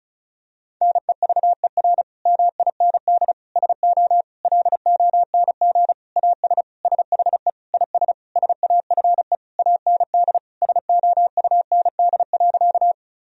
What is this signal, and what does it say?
Morse code, heard as a keyed tone, about 36 wpm, 700 Hz. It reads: NEVER MIND SO LONG AS SHE IS SAFE AND SOUND.
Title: Sentences from Top 1000 Words - Review Part 1 - 35wpm